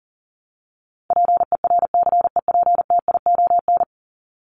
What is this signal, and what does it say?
Morse code, heard as a keyed tone, about 40 wpm, 700 Hz. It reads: PERCEPTION
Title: Top 1751-2000 Words 40wpm